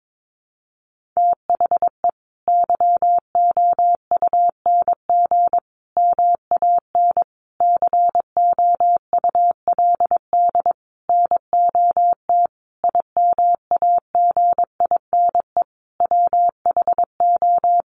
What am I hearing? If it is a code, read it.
Morse code, heard as a keyed tone, about 22 wpm, 700 Hz. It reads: THE YOUNG MAN COULD NOT IMAGINE WHO
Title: Sentences from Top 1400 Words 22wpm